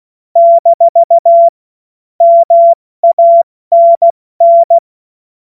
Morse code, heard as a keyed tone, about 16 wpm, 680 Hz. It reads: - MANN
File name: Morsekurs / Witze / Witz 01. November 2022 70 F80